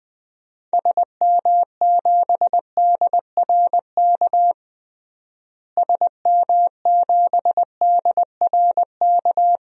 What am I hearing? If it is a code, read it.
Morse code, heard as a keyed tone, about 21 wpm, 700 Hz. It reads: SM7DRK SM7DRK
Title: DXCC-100 Call Signs - 6 Characters - Repeated 2x - 20wpm